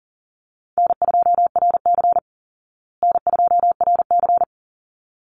Morse code, heard as a keyed tone, about 40 wpm, 700 Hz. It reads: N2RC N2RC
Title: Easy Call Signs Repeated 2x - 40wpm